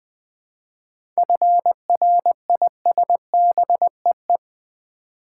Morse code, heard as a keyed tone, about 21 wpm, 700 Hz. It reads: FRISBEE
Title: Difficult Words with Hints - 20wpm